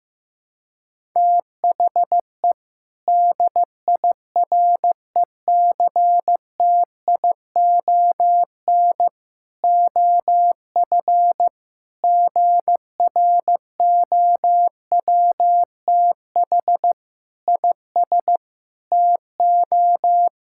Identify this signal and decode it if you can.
Morse code, heard as a keyed tone, about 15 wpm, 700 Hz. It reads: THE DIRECTION OF GROWTH IS TO
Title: Sentences from Top 1000 Words - Review Part 5 - 15wpm